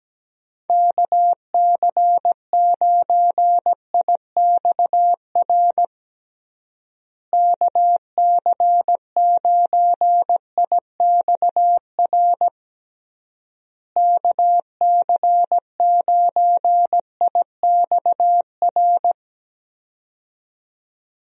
Morse code, heard as a keyed tone, about 17 wpm, 700 Hz. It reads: KC9IXR KC9IXR KC9IXR
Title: USA Call Signs - 6 Characters - Repeated 3x - 17wpm